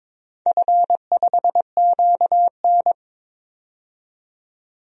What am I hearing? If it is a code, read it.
Morse code, heard as a keyed tone, about 22 wpm, 700 Hz. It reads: F5QN
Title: DXCC-100 Call Signs - 4 Characters - 22wpm